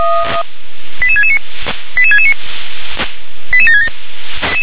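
Five-tone selective call (ZVEI), heard as quick runs of five tones, tones of about 70 ms. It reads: 88669 88669 88767
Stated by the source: Five-tone ZVEI selective call 88713, received by SDR.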